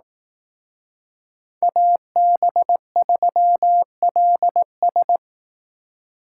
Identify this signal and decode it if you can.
Morse code, heard as a keyed tone, about 19 wpm, 700 Hz. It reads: AB3LS
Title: Speed Racing - USA 2x2 Call Signs 15wpm